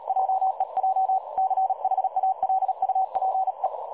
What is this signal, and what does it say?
Morse code, heard as a keyed tone, about 38 wpm, 760 Hz. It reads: JEJ CHALUPE
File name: dlazaawansowanych / nagraniazpasma / websdr recording 3565.0kHz